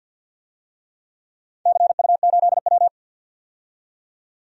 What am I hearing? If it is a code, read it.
Morse code, heard as a keyed tone, about 50 wpm, 700 Hz. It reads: CU8W
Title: DXCC-100 Call Signs - 4 Characters - 50wpm